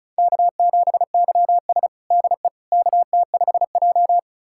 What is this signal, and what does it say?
Morse code, heard as a keyed tone, about 34 wpm, 700 Hz. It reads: K7YS DE KT5J
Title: USA Callsign DE Callsign - 4 Characters - 35wpm